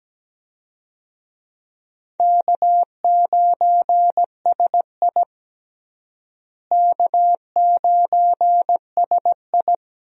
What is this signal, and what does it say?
Morse code, heard as a keyed tone, about 17 wpm, 700 Hz. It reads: K9SI K9SI
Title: USA Call Signs - 4 Characters - Repeated 2x - 17wpm